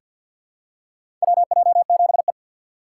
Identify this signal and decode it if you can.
Morse code, heard as a keyed tone, about 48 wpm, 700 Hz. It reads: WJ7E
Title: USA Call Signs - 4 Characters - 6x Character Spacing - 50wpm